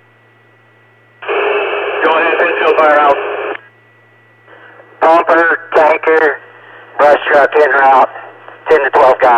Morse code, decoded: none

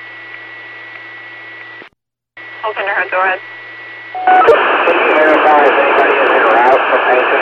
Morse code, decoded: none

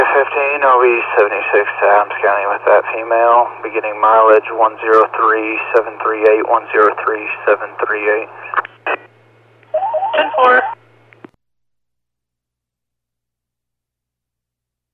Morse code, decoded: none

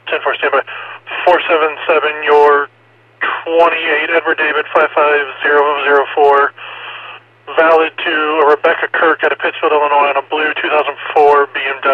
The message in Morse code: none

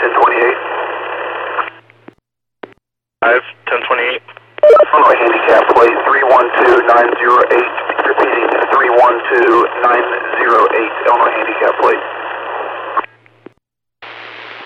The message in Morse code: none

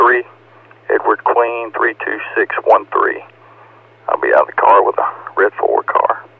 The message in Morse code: none

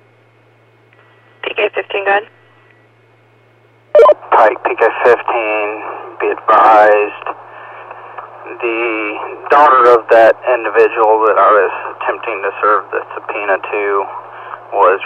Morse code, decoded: none

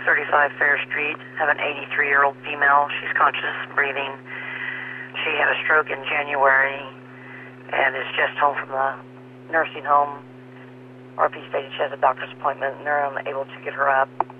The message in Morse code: none

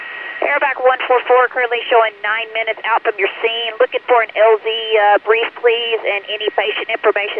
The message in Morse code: none

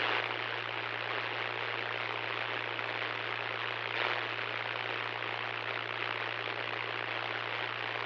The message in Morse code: none

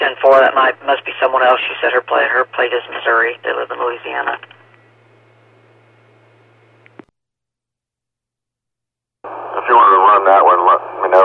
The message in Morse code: none

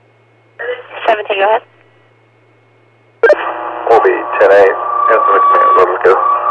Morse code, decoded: none

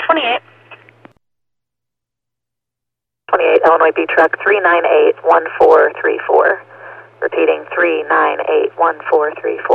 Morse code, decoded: none

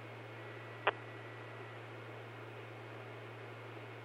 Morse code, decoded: none

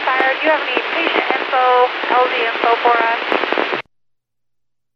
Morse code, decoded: none

